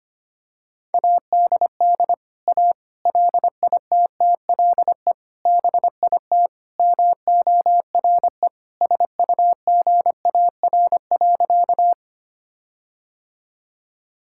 Morse code, heard as a keyed tone, about 25 wpm, 700 Hz. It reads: ADD A LITTLE BIT MORE SUGAR.
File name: practice / 25wpm / Sentences with 4-Word Phrases - 25wpm